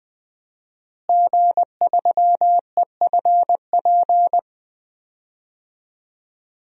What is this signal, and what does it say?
Morse code, heard as a keyed tone, about 20 wpm, 700 Hz. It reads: G3EFP
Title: DXCC-100 Call Signs - 5 Characters - Repeated 3x - 20wpm